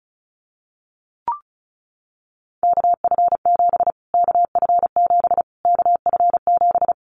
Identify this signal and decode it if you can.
Morse code, heard as a keyed tone, about 34 wpm, 700 Hz. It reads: KF7 KF7 KF7
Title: Callsign Prefixes for Top 25 DXCC - Repeated 3x - 35wpm